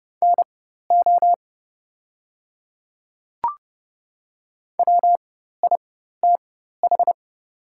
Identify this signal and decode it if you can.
Morse code, heard as a keyed tone, about 30 wpm, spaced as for 14 wpm, 700 Hz. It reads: NO WITH
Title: 4x Character Spacing - Top 100 Words - 30wpm